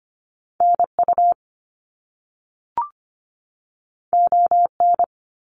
Morse code, heard as a keyed tone, about 25 wpm, 700 Hz. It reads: NU ON
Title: Canadian Provinces and Territories 25wpm